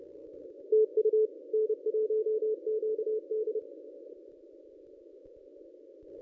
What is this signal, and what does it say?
Morse code, heard as a keyed tone, about 30 wpm, 430 Hz. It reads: TU N1QD
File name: contests / ss-cw-k5tr-op-n5rz -1000